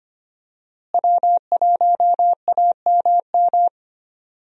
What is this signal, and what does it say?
Morse code, heard as a keyed tone, about 25 wpm, 700 Hz. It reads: W1AMM
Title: USA Call Signs - 5 Characters - Repeated 3x - 25wpm